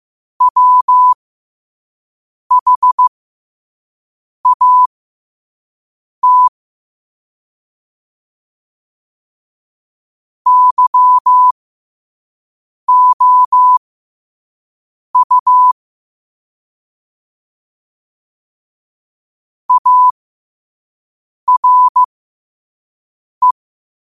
Morse code, heard as a keyed tone, about 15 wpm, spaced as for 5 wpm, 1000 Hz. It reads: WHAT YOU ARE
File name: qotdfiles / 5wpm